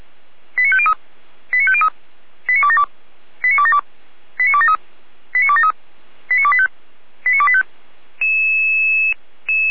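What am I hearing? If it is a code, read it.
Five-tone selective call (ZVEI), heard as quick runs of five tones, tones of about 70 ms. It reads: 89503 89503 89272 89272 89273 89273 89285 89285
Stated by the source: ZVEI five-tone call received by SDR, from the Feuerwehr Orlamünde audio server.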